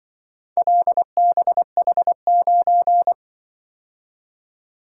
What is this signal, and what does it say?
Morse code, heard as a keyed tone, about 24 wpm, 700 Hz. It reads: LBH9
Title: Speed-Racing - Four Letter-Number Combinations - 20wpm